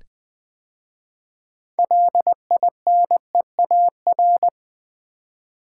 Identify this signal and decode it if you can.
Morse code, heard as a keyed tone, about 20 wpm, 700 Hz. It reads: LINEAR
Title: 6 Letter Words - First 3 Letters Given - 20wpm